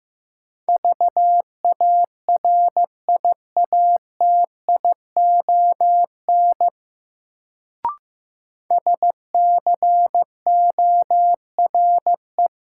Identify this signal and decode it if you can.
Morse code, heard as a keyed tone, about 15 wpm, 700 Hz. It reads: VARIATION SCORE
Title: Sentences from Top 1600 Words - New Word Drill - 15wpm